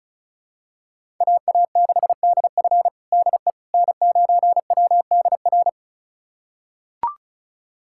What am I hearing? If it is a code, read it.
Morse code, heard as a keyed tone, about 34 wpm, 700 Hz. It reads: AA6DF DE N9WDR
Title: USA Callsign DE Callsign - 5 Characters - 35wpm